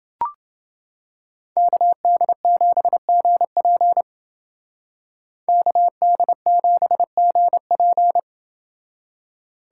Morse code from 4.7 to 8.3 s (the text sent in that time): KD7GP